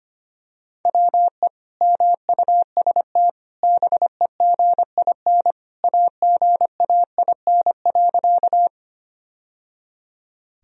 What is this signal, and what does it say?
Morse code, heard as a keyed tone, about 25 wpm, 700 Hz. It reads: WE MUST BEGIN AGAIN.
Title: Sentences from Top 1000 Words - Review Part 1 - 25wpm